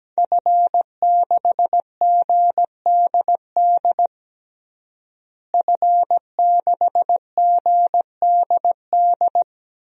Morse code, heard as a keyed tone, about 17 wpm, 700 Hz. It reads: F6GDD F6GDD
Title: DXCC-100 Call Signs - 5 Characters - Repeated 3x - 17wpm